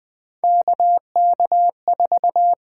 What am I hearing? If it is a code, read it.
Morse code, heard as a keyed tone, about 20 wpm, 700 Hz. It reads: KK4
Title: USA Call Signs - 6 Characters - 8x Character Spacing - 20wpm